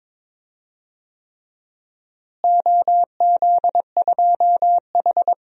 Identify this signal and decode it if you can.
Morse code, heard as a keyed tone, about 22 wpm, 700 Hz. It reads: OZ2H
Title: DXCC-100 Call Signs - 4 Characters - Repeated 3x - 22wpm